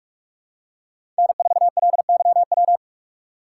Morse code, heard as a keyed tone, about 44 wpm, 700 Hz. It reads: N4LYW
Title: USA Call Signs - 5 Characters - 6x Character Spacing - 45wpm